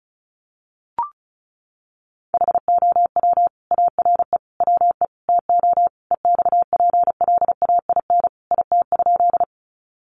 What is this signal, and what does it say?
Morse code, heard as a keyed tone, about 34 wpm, 700 Hz. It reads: HOW ARE WE TO EXPLAIN IT?
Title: Sentences from Top 1100 Words 35wpm